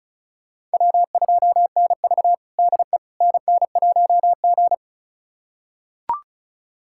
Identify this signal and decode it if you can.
Morse code, heard as a keyed tone, about 36 wpm, 700 Hz. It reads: W2NV DE NN1G
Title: USA Callsign DE Callsign - 4 Characters - 35wpm